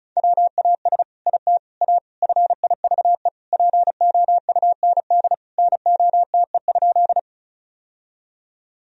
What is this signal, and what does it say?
Morse code, heard as a keyed tone, about 34 wpm, 700 Hz. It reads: WAS IT A FIVE POUND NOTE?
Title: Sentences from Top 1000 Words - Review Part 2 - 35wpm